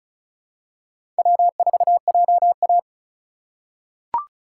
Morse code, heard as a keyed tone, about 36 wpm, 700 Hz. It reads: W4JA
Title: USA Call Signs - 4x Character Spacing - 35wpm